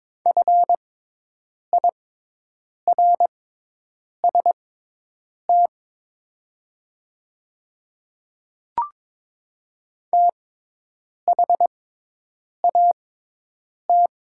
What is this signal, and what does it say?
Morse code, heard as a keyed tone, about 22 wpm, spaced as for 8 wpm, 700 Hz. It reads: FIRST THAT
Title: 6x Character Spacing - Top 200 Words - 22wpm